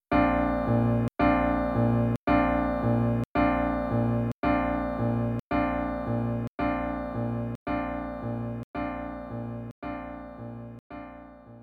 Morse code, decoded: none